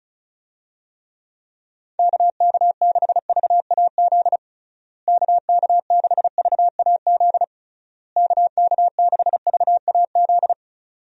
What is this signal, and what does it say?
Morse code, heard as a keyed tone, about 36 wpm, 700 Hz. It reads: KK6VAZ KK6VAZ KK6VAZ
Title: USA Call Signs - 6 Characters - Repeated 3x - 35wpm